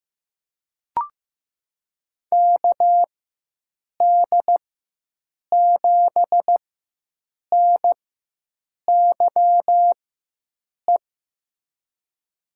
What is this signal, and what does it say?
Morse code, heard as a keyed tone, about 15 wpm, spaced as for 7 wpm, 700 Hz. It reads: KD7NYE
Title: USA Call Signs - 6 Characters - 4x Character Spacing - 15wpm